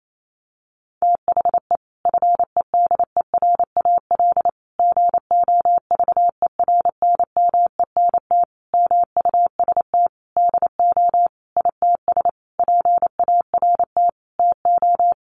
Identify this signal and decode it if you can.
Morse code, heard as a keyed tone, about 28 wpm, 700 Hz. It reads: THE FEDERAL GOVERNMENT MUST DO ITS PART TO
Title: Sentences from Top 1100 Words 28wpm